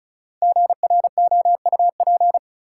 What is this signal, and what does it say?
Morse code, heard as a keyed tone, about 36 wpm, 700 Hz. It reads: GROUP
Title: Sentences from Top 1000 Words - Review Part 1 - New Word Drill - 35wpm